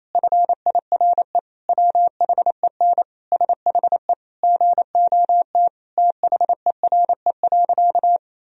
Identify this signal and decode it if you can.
Morse code, heard as a keyed tone, about 28 wpm, 700 Hz. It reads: FIRE WHEN SHE GOT THERE.
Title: Sentences from Top 1000 Words - Review Part 1 - 28wpm